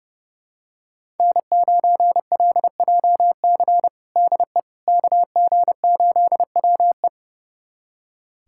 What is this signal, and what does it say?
Morse code, heard as a keyed tone, about 30 wpm, 700 Hz. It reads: N9LJC DE KG8WE
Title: USA Callsign DE Callsign - 5 Characters - 30wpm